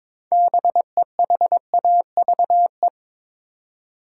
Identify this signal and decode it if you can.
Morse code, heard as a keyed tone, about 21 wpm, 700 Hz. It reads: BEHAVE
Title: Six Letter Words - 22wpm